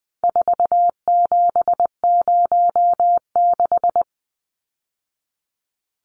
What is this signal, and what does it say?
Morse code, heard as a keyed tone, about 20 wpm, 700 Hz. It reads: 4706
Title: Numbers - 4-Digit 20wpm